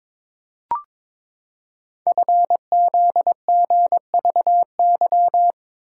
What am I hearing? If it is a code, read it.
Morse code, heard as a keyed tone, about 22 wpm, 700 Hz. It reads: FZGVY
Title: Code Groups - 5 Letters - 22wpm